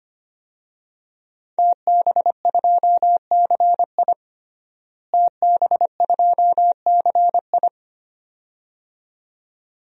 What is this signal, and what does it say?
Morse code, heard as a keyed tone, about 25 wpm, 700 Hz. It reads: TB2CI TB2CI
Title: DXCC-100 Call Signs - 5 Characters - Repeated 2x - 25wpm